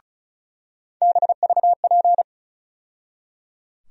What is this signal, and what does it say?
Morse code, heard as a keyed tone, about 36 wpm, 700 Hz. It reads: BVP
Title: USA Call Sign Postfixes - 3 Letters 35wpm